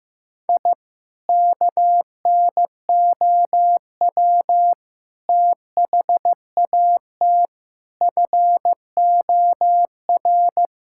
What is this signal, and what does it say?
Morse code, heard as a keyed tone, about 15 wpm, 700 Hz. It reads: I KNOW THAT FOR
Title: Sentences from Top 1000 Words - Review Part 5 - 15wpm